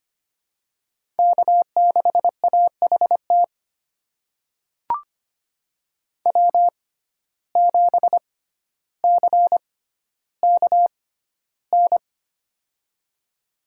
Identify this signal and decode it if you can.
Morse code, heard as a keyed tone, about 25 wpm, spaced as for 13 wpm, 700 Hz. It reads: K6AHT W7CKN
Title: USA Call Signs - 5 Characters - 6x Character Spacing - 25wpm